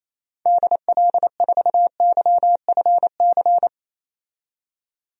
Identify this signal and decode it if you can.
Morse code, heard as a keyed tone, about 28 wpm, 700 Hz. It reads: DL4YFC
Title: DXCC-100 Call Signs - 6 Characters - 28wpm